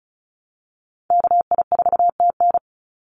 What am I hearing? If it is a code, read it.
Morse code, heard as a keyed tone, about 34 wpm, 700 Hz. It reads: KI4TN
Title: NA SOTA Activator Callsigns 35wpm